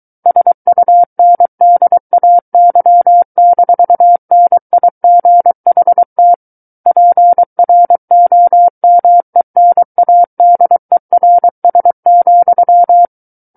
Morse code, heard as a keyed tone, about 23 wpm, 700 Hz. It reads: SUNDAY-NIGHT PROMENADERS,
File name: Book One / 14-Chapter 14 - 23 WPM